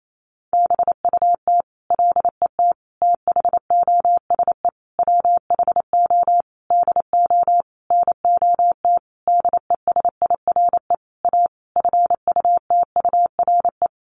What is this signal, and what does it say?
Morse code, heard as a keyed tone, about 28 wpm, 700 Hz. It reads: BUT LET THOSE WHO DO NOT DESIRE A FUTURE
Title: Sentences from Top 1500 Words 28wpm